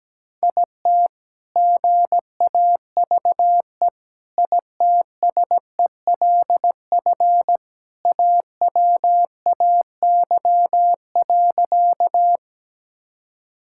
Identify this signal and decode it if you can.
Morse code, heard as a keyed tone, about 17 wpm, 700 Hz. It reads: IT GAVE ITSELF AWAY.
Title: Sentences from Top 1000 Words - Review Part 5 - 17wpm